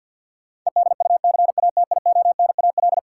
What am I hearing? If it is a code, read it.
Morse code, heard as a keyed tone, about 51 wpm, 700 Hz. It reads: EDUCATIONAL
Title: Sentences from Top 1200 Words - New Word Drill - 50wpm